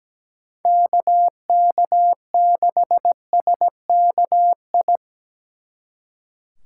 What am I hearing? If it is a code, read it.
Morse code, heard as a keyed tone, about 20 wpm, 700 Hz. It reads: KK6SKI
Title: Speed Racing - USA Call Signs 17wpm